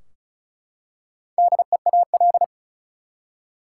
Morse code, heard as a keyed tone, about 36 wpm, 700 Hz. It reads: DEAL
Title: Four Letter Words - 35wpm